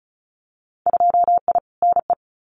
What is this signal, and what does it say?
Morse code, heard as a keyed tone, about 36 wpm, 700 Hz. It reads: 2I NE
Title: Winter Field Day Exchanges Encore 35wpm